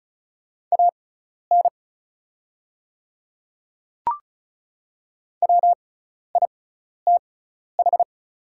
Morse code, heard as a keyed tone, about 34 wpm, spaced as for 12 wpm, 700 Hz. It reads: AN WITH
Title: 6x Character Spacing - Top 100 Words - 35wpm